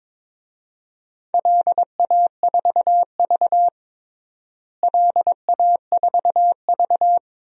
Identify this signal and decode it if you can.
Morse code, heard as a keyed tone, about 22 wpm, 700 Hz. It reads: LA4V LA4V
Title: DXCC-100 Call Signs - 4 Characters - Repeated 3x - 22wpm